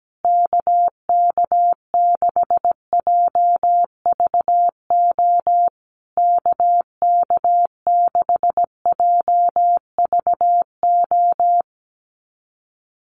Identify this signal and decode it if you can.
Morse code, heard as a keyed tone, about 17 wpm, 700 Hz. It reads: KK6JVO KK6JVO
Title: USA Call Signs Repeated 3x 17wpm